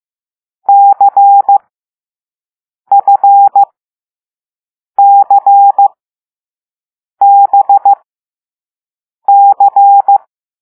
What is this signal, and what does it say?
Morse code, heard as a keyed tone, about 15 wpm, spaced as for 5 wpm, 800 Hz. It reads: CFCBC
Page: Play Lesson 6 C - all letters in previous lessons